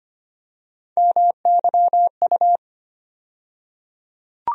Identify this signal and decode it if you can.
Morse code, heard as a keyed tone, about 25 wpm, 700 Hz. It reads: MYU
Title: USA Call Sign Postfixes - 3 Letters 25wpm